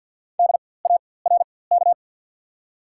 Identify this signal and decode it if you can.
Morse code, heard as a keyed tone, about 54 wpm, spaced as for 24 wpm, 700 Hz. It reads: DARK
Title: 4x Character Spacing - Top 500 Words - 50wpm